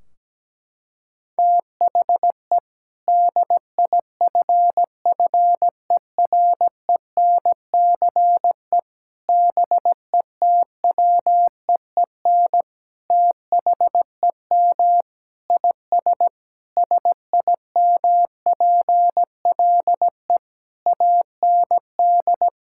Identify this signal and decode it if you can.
Morse code, heard as a keyed tone, about 17 wpm, 700 Hz. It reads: THE DIFFERENCE BETWEEN THEM IS SIMPLE AND
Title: Sentences from Top 1600 Words 17wpm